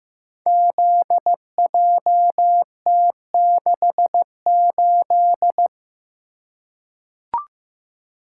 Morse code, heard as a keyed tone, about 15 wpm, 700 Hz. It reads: ZJT68
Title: Code Groups - 5 Letter:Numbers - 15wpm